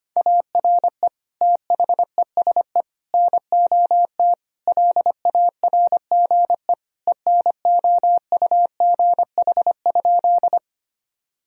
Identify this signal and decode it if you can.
Morse code, heard as a keyed tone, about 25 wpm, 700 Hz. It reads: ARE THESE NOT LARGE ENOUGH?
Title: Sentences from Top 1000 Words - Review Part 1 - 25wpm